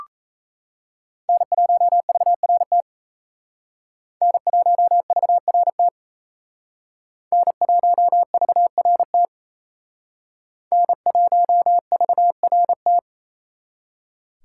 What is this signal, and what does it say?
Morse code, heard as a keyed tone, about 35 wpm, 700 Hz. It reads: N1VRT N1VRT N1VRT N1VRT
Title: Speed Racing - USA 1x3 Call Signs 28wpm